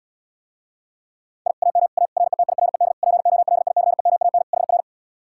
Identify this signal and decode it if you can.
Morse code, heard as a keyed tone, about 80 wpm, 700 Hz. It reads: I AM A RETIRED POLICEMAN HR